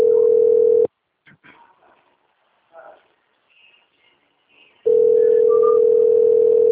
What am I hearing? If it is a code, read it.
Ringback tone (North America), 440 + 480 Hz.
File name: out- - - - - .8549